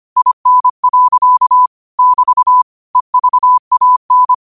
Morse code, heard as a keyed tone, about 25 wpm, 1000 Hz. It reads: IN. <BT> EVAN